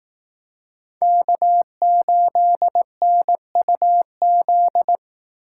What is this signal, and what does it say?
Morse code, heard as a keyed tone, about 18 wpm, 700 Hz. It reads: K8NUZ
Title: Speed Racing - USA 1x3 Call Signs 15wpm